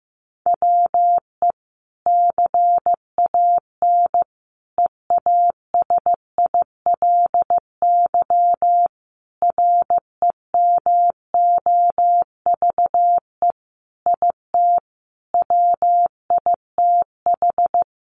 Morse code, heard as a keyed tone, about 14 wpm, 700 Hz. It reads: WE CAN EASILY REMOVE IT WITH
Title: Sentences from Top 1200 Words 15wpm